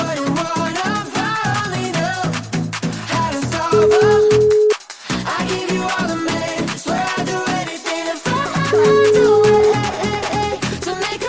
Ringback tone (Europe, 425 Hz), with noise.